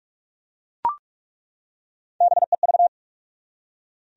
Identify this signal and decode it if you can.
Morse code, heard as a keyed tone, about 48 wpm, 700 Hz. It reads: BEV
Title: USA Common Names - Max 3 Letters - 45wpm